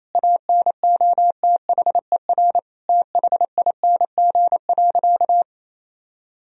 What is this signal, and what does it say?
Morse code, heard as a keyed tone, about 27 wpm, 700 Hz. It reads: ANOTHER THING.
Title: Sentences from Top 1000 Words - Review Part 3 - 28wpm